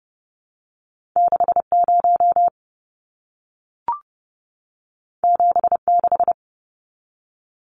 Morse code, heard as a keyed tone, about 30 wpm, 700 Hz. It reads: 60 76
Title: Numbers - 2-Digit 30wpm